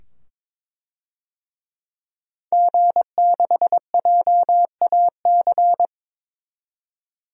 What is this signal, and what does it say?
Morse code, heard as a keyed tone, about 22 wpm, 700 Hz. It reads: G6JAC
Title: DXCC-100 Call Signs - 5 Characters - 22wpm